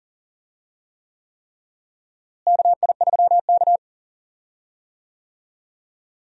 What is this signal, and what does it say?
Morse code, heard as a keyed tone, about 40 wpm, 700 Hz. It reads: KI3K
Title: USA Call Signs - 4 Characters - Repeated 3x - 40wpm